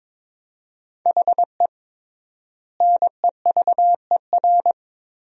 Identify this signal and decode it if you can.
Morse code, heard as a keyed tone, about 21 wpm, 700 Hz. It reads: HE NEVER
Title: 3x Word-Spacing - Sentences from Top 400 Words - 22wpm